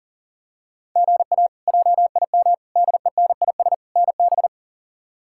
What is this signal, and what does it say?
Morse code, heard as a keyed tone, about 40 wpm, 700 Hz. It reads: GA JIM DENIS NB
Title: K1USN SST QSO Practice - 40wpm